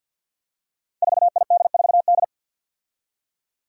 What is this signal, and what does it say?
Morse code, heard as a keyed tone, about 52 wpm, 700 Hz. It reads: 4ID4D
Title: Five Challenging Letter-Number Combinations 50wpm